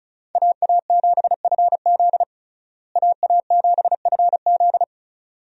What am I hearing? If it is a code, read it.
Morse code, heard as a keyed tone, about 36 wpm, 700 Hz. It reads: AA7FZ AA7FZ
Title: USA Call Signs - 5 Characters - Repeated 2x - 35wpm